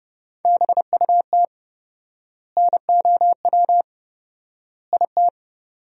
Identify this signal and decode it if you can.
Morse code, heard as a keyed tone, about 30 wpm, 700 Hz. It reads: BUT NOW IT
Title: 4x Word-Spacing - Sentences from Top 300 Words - 30wpm